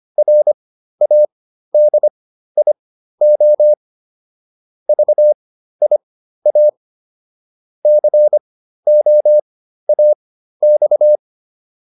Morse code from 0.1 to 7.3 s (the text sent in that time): RADIO VIA